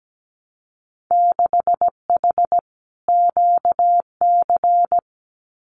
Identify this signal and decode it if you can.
Morse code, heard as a keyed tone, about 17 wpm, 700 Hz. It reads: 6H QC
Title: Winter Field Day Exchanges Encore 17wpm